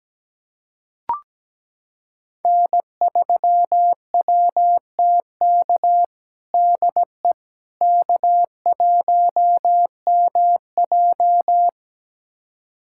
Morse code, heard as a keyed tone, about 17 wpm, 700 Hz. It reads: N3WTK DE K1MJ
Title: USA Callsign DE Callsign 17wpm